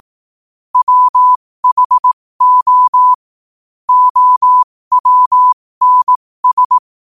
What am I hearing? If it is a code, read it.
Morse code, heard as a keyed tone, about 18 wpm, spaced as for 15 wpm, 1000 Hz. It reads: WHO OWNS
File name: qotdfiles / 15wpm